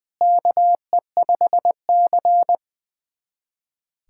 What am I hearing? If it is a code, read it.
Morse code, heard as a keyed tone, about 20 wpm, 700 Hz. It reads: KE5C
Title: USA Callsign DE Callsign - 4 Characters - 20wpm